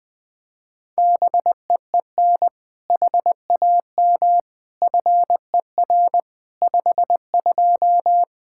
Morse code, heard as a keyed tone, about 20 wpm, 700 Hz. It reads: BEEN HAM FER 52
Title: Practice QSO - Exchange 3 - More Info - 20wpm